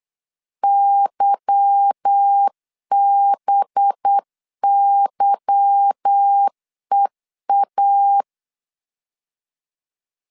Morse code, heard as a keyed tone, about 8 wpm, 800 Hz. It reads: YBYEA
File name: LEZIONI / 8 YJ45 / 8 Y 8wpm 3245